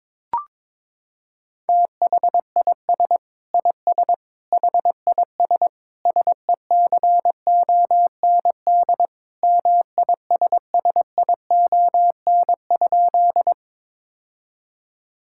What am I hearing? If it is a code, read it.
Morse code, heard as a keyed tone, about 22 wpm, 700 Hz. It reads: THIS IS HIS SECOND MISSION?
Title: Sentences from Top 1600 Words 22wpm